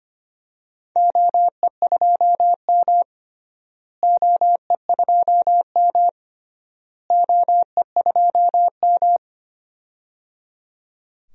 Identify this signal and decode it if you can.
Morse code, heard as a keyed tone, about 25 wpm, 700 Hz. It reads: OE2M OE2M OE2M
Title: DXCC-100 Call Signs - 4 Characters - Repeated 3x - 25wpm